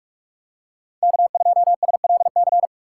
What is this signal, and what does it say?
Morse code, heard as a keyed tone, about 46 wpm, 700 Hz. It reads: K2SLC
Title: USA Call Signs - 5 Characters - 3x Character Spacing - 45wpm